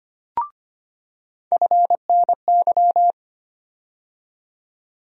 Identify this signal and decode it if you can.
Morse code, heard as a keyed tone, about 25 wpm, 700 Hz. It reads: FNY
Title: USA Call Sign Postfixes - 3 Letters 25wpm